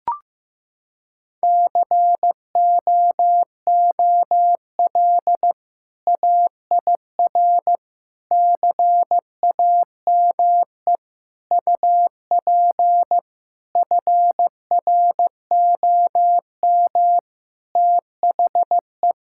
Morse code, heard as a keyed tone, about 16 wpm, 700 Hz. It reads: COOL AIR CAME UP FROM THE
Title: Sentences from Top 1600 Words 15wpm